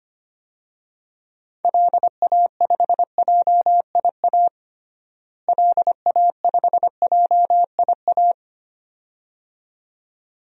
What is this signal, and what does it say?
Morse code, heard as a keyed tone, about 25 wpm, 700 Hz. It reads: LA5JIA LA5JIA
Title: DXCC-100 Call Signs - 6 Characters - Repeated 2x - 25wpm